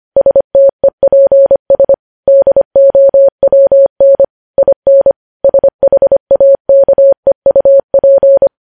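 Morse code, heard as a keyed tone, about 25 wpm, 550 Hz. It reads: STEPS DOWN IN SHAKEUP
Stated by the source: Morse code news headlines at 25 wpm.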